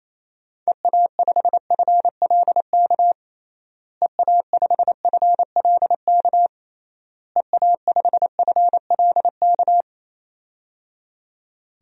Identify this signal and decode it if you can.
Morse code, heard as a keyed tone, about 28 wpm, 700 Hz. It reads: EA5FLK EA5FLK EA5FLK
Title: DXCC-100 Call Signs - 6 Characters - Repeated 3x - 28wpm